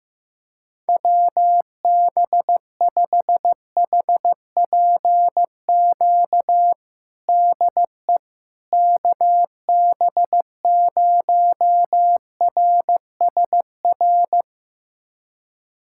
Morse code, heard as a keyed tone, about 15 wpm, 700 Hz. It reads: WB5HPQ DE KB0RSR
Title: USA Callsign DE Callsign - 6 Characters - 15wpm